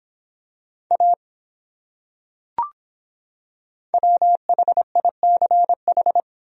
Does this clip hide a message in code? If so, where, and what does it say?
Yes: Morse code from 0.9 to 6.2 s, heard as a keyed tone, about 25 wpm, 700 Hz. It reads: A WHICH